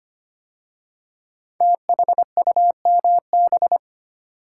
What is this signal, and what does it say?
Morse code, heard as a keyed tone, about 25 wpm, 700 Hz. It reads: THUMB
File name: 5 Letter Words - First 3 Letters Given - 25wpm